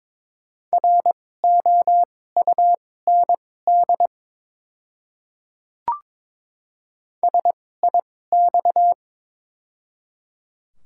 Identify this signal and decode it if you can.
Morse code, heard as a keyed tone, about 21 wpm, spaced as for 16 wpm, 700 Hz. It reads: ROUND SIX
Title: 2x Character Spacing - Top 500 Words - 22wpm